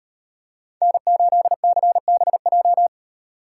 Morse code, heard as a keyed tone, about 38 wpm, 700 Hz. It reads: N8CBJ